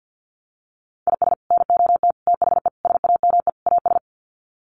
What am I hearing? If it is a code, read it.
Morse code, heard as a keyed tone, about 50 wpm, 700 Hz. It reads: IS NOT THE SAME AS
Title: Sets of 5 Words Encore 50wpm